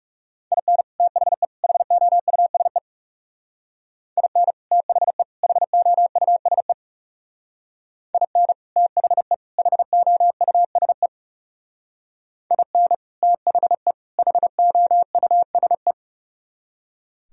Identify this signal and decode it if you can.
Morse code, heard as a keyed tone, about 38 wpm, 700 Hz. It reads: IN THE HOUSE IN THE HOUSE IN THE HOUSE IN THE HOUSE